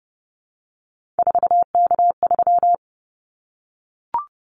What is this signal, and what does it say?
Morse code, heard as a keyed tone, about 30 wpm, 700 Hz. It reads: 4K3
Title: Callsign Prefixes for Top 100 DXCC - 30wpm